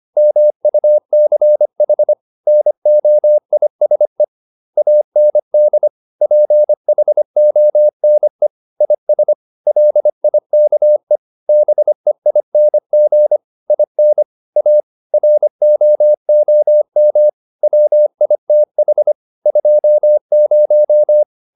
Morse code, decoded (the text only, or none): MUCH NOISE AND PHONE IS LIKE BEING IN A ROOM WITH 20